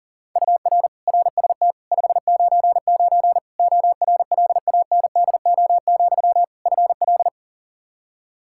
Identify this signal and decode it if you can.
Morse code, heard as a keyed tone, about 40 wpm, 700 Hz. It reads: UR RST 599 ORLANDO, FL
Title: Practice QSO - Exchange 1 - Beginning - 40wpm